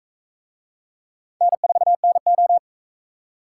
Morse code, heard as a keyed tone, about 44 wpm, 700 Hz. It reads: N4NO